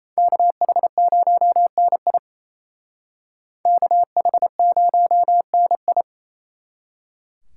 Morse code, heard as a keyed tone, about 36 wpm, 700 Hz. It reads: KH0NI KH0NI